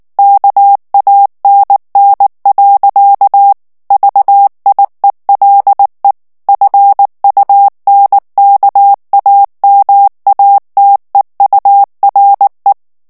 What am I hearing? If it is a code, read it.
Morse code, heard as a keyed tone, about 19 wpm, 800 Hz. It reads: KANN. VIELE FUNKAMATEURE
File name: morse mp3 / ALLEMAND texte clair 2 vitesse 18 mots minute